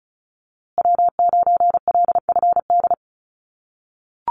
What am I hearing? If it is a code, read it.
Morse code, heard as a keyed tone, about 34 wpm, 700 Hz. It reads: W9LFD
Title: USA Call Signs Repeated 2x 35wpm